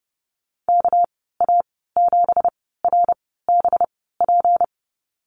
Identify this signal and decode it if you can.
Morse code, heard as a keyed tone, about 30 wpm, spaced as for 17 wpm, 700 Hz. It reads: KA7RBP
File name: Lesson 37g - Letter x - Call Signs 30wpm